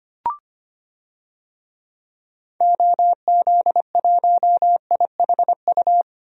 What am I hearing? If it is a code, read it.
Morse code, heard as a keyed tone, about 25 wpm, 700 Hz. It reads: OZ1IHU
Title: DXCC-100 Call Signs - 6 Characters - Repeated 2x - 25wpm